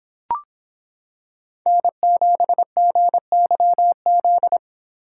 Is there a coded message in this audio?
Yes: Morse code from 1.7 to 4.6 s, heard as a keyed tone, about 26 wpm, 700 Hz. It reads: N7GYZ